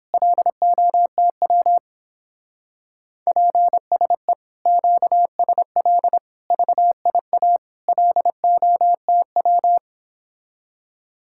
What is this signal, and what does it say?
Morse code, heard as a keyed tone, about 29 wpm, 700 Hz. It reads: LOTW PSE QSL VIA LOTW